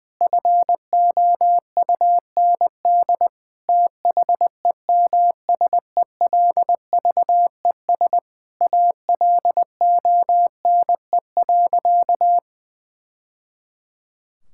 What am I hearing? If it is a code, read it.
Morse code, heard as a keyed tone, about 21 wpm, 700 Hz. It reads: FOUND THEMSELVES ALONE.
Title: Sentences from Top 1000 Words - Review Part 5 - 20wpm